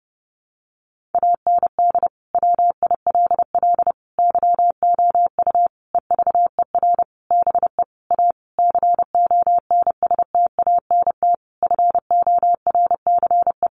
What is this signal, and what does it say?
Morse code, heard as a keyed tone, about 30 wpm, 700 Hz. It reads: AND WILL YOU EVER BE A CONSTANT FORCE
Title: Sentences from Top 1500 Words 30wpm